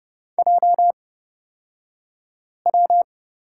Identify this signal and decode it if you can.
Morse code, heard as a keyed tone, about 30 wpm, spaced as for 5 wpm, 690 Hz. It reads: JW
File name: Lesson 36d - BT - Two Characters 30wpm